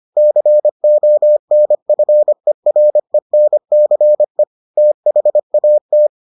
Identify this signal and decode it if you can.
Morse code, heard as a keyed tone, about 25 wpm, 600 Hz. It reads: CONFERENCE THAT